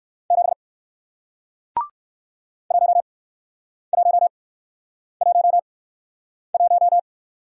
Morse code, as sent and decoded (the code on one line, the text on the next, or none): --... / .---- .---- .---- .----
7 1111